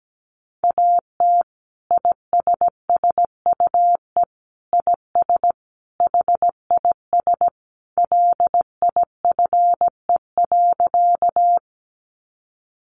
Morse code, heard as a keyed tone, about 17 wpm, 700 Hz. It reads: AT ISSUE IS HIS LIFE.
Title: Sentences from Top 900 Words 17wpm